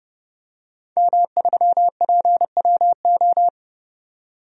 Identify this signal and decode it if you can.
Morse code, heard as a keyed tone, about 32 wpm, 700 Hz. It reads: M3PWO
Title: DXCC-100 Call Signs - 5 Characters - 30wpm